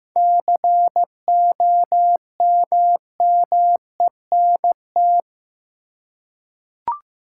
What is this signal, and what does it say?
Morse code, heard as a keyed tone, about 15 wpm, 700 Hz. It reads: COMMENT
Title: Seven Letter Words - 15wpm